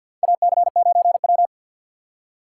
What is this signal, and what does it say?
Morse code, heard as a keyed tone, about 47 wpm, 700 Hz. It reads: AC9W
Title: USA Call Signs - 4 Characters - 6x Character Spacing - 50wpm